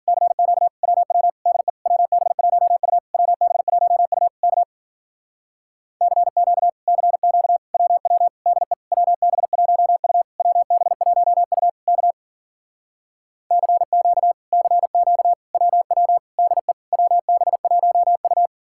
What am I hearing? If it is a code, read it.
Morse code, heard as a keyed tone, about 54 wpm, 700 Hz. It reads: CQ WW DE WB1U WB1U K CQ CQ WW DE WB1U WB1U K CQ CQ WW DE WB1U